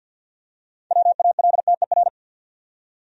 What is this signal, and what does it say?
Morse code, heard as a keyed tone, about 47 wpm, 700 Hz. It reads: WALTER
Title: USA Common Names - 50wpm